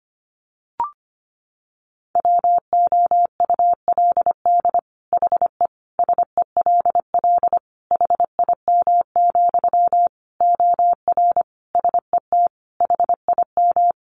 Morse code, heard as a keyed tone, about 25 wpm, 700 Hz. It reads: WOULD HE SELL HIM, OR SET HIM
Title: Sentences from Top 1100 Words 25wpm